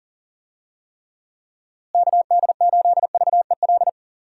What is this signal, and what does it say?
Morse code, heard as a keyed tone, about 40 wpm, 700 Hz. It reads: KD8VEL
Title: USA Call Signs - 6 Characters - Repeated 3x - 40wpm